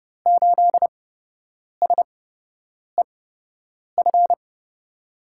Morse code, heard as a keyed tone, about 31 wpm, spaced as for 8 wpm, 700 Hz. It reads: 8SEF
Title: USA Call Signs - 6 Characters - 8x Character Spacing - 30wpm